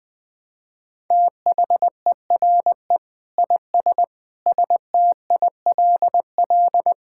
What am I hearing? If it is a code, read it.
Morse code, heard as a keyed tone, about 20 wpm, 700 Hz. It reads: THERE IS STILL
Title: Sentences with 9-Word Phrases - Spoken Beginning - 20wpm